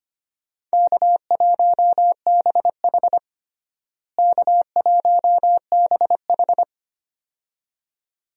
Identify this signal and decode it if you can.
Morse code, heard as a keyed tone, about 25 wpm, 700 Hz. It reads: K1BH K1BH
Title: USA Call Signs - 4 Characters - Repeated 2x - 25wpm